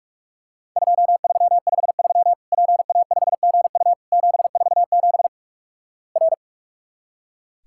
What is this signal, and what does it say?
Morse code, heard as a keyed tone, about 46 wpm, 700 Hz. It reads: 2353 PA5GU 747 R
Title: Worked All Europe DX Contest - QTC Exchange - 45wpm